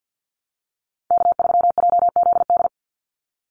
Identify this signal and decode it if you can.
Morse code, heard as a keyed tone, about 50 wpm, 700 Hz. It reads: K3JZD
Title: NA SOTA Chaser Callsigns 50wpm